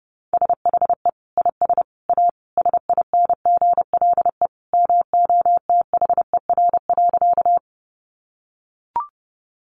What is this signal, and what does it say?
Morse code, heard as a keyed tone, about 30 wpm, 700 Hz. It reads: SHE IS A SINGLE MOTHER.
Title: Sentences from Top 700 Words 30wpm